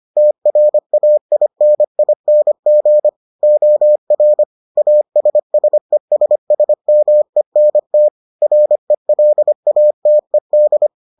Morse code, heard as a keyed tone, about 25 wpm, 600 Hz. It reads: TRAINING OR ASSESSMENT RELATED